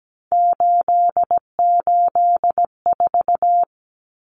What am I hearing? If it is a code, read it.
Morse code, heard as a keyed tone, about 17 wpm, 700 Hz. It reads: 884
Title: Common Cut Numbers - 3-Digit 17wpm